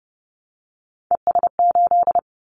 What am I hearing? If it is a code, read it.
Morse code, heard as a keyed tone, about 30 wpm, 700 Hz. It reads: ES8
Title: Callsign Prefixes for Top 100 DXCC - 30wpm